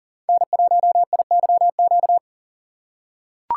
Morse code, heard as a keyed tone, about 38 wpm, 700 Hz. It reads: N1IYQ
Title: USA Call Signs - 4x Character Spacing - 40wpm